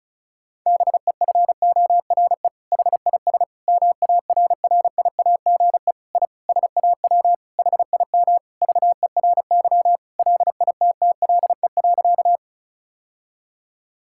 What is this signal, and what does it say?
Morse code, heard as a keyed tone, about 34 wpm, 700 Hz. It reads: BEFORE HIS MARRIAGE I SAW HIM VERY LITTLE.
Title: Sentences from Top 1000 Words - Review Part 5 - 35wpm